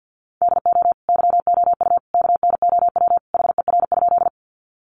Morse code, heard as a keyed tone, about 50 wpm, 700 Hz. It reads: DO YOU KNOW HER?
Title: Sentences from Top 100 Words 50wpm